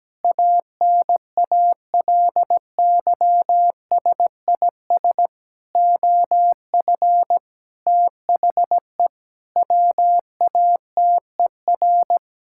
Morse code, heard as a keyed tone, about 17 wpm, 700 Hz. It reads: ANALYSIS OF THE WATER
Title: Sentences from Top 1000 Words - Review Part 5 - 17wpm